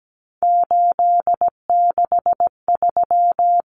Morse code, heard as a keyed tone, about 17 wpm, 700 Hz. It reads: 863
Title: Common Cut Numbers - 3-Digit 17wpm